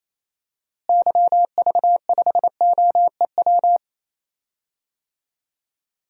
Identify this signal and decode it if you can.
Morse code, heard as a keyed tone, about 28 wpm, 700 Hz. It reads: YV5OEW
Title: DXCC-100 Call Signs - 6 Characters - Repeated 2x - 28wpm